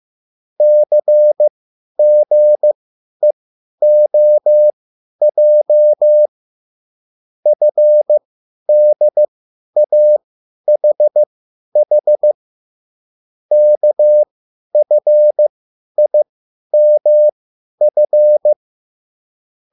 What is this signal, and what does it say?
Morse code, heard as a keyed tone, about 15 wpm, spaced as for 10 wpm, 600 Hz. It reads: CGEOJ FDAHH KFIMF
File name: QSL / CW lesson